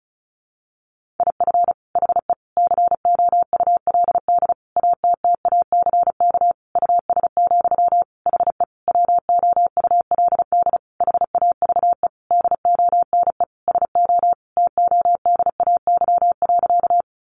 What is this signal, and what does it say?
Morse code, heard as a keyed tone, about 34 wpm, 700 Hz. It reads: IF HE COULD ATTACK US, HE WOULD HAVE DONE SO TODAY.
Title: Sentences from Top 1500 Words 35wpm